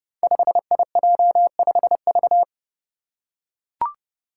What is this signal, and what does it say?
Morse code, heard as a keyed tone, about 30 wpm, 700 Hz. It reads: 5IJ5V
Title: Five Challenging Letter-Number Combinations 30wpm